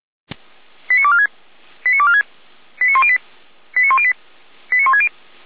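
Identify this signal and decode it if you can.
Five-tone selective call (ZVEI), heard as quick runs of five tones, tones of about 80 ms. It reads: 89246 89246 89108 89108 89160
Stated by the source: ZVEI five-tone call received by SDR, fire brigade Orlamünde (feuerwehr-orlamuende).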